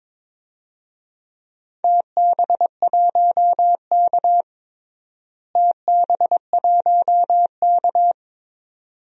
Morse code, heard as a keyed tone, about 21 wpm, 700 Hz. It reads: TB1K TB1K
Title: DXCC-100 Call Signs - 4 Characters - Repeated 2x - 22wpm